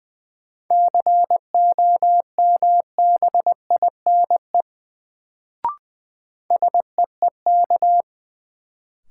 Morse code, heard as a keyed tone, about 20 wpm, 700 Hz. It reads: COMBINE SEEK
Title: Sentences from Top 1600 Words - New Word Drill - 20wpm